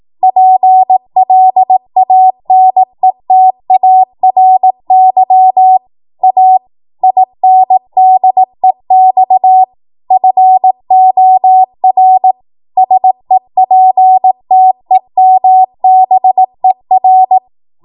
Morse code, heard as a keyed tone, about 18 wpm, 750 Hz. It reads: PLANETARY A INDEX FOR SEPTEMBER
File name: Morse / 18WPM